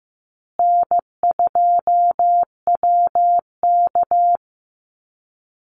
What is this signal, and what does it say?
Morse code, heard as a keyed tone, about 15 wpm, 700 Hz. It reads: N2WK
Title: Easy Call Signs Repeated 2x - 15wpm